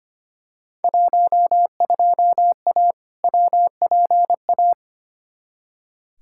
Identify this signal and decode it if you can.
Morse code, heard as a keyed tone, about 25 wpm, 700 Hz. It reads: 12A WPA
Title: Field Day Exchanges Encore - 25wpm